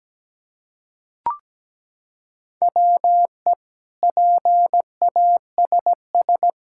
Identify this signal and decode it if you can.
Morse code, heard as a keyed tone, about 17 wpm, 700 Hz. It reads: WE PASS
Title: Sentences from Top 1600 Words 17wpm